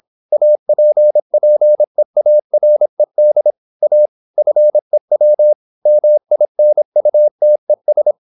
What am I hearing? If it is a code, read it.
Morse code, heard as a keyed tone, about 26 wpm, 600 Hz. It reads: APPEARED A FEW MINUTES
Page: Never 26wpm | CW med Gnister